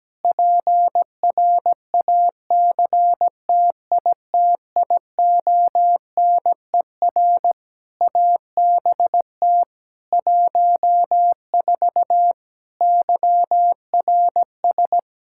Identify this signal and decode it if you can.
Morse code, heard as a keyed tone, about 17 wpm, 700 Hz. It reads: PRACTITIONER ABT 14 YRS
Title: Practice QSO - Exchange 3 - More Info - 17wpm